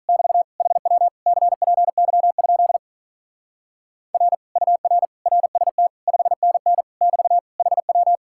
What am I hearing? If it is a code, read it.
Morse code, heard as a keyed tone, about 45 wpm, 700 Hz. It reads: <BT> HW CPY? R UR RST 5NN <BT> HW